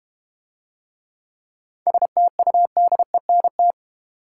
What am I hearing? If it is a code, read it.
Morse code, heard as a keyed tone, about 32 wpm, 690 Hz. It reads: STUDENT